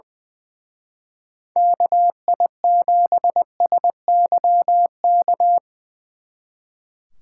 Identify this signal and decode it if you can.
Morse code, heard as a keyed tone, about 22 wpm, 700 Hz. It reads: KI7SYK
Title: Speed Racing - USA 2x3 Call Signs 20wpm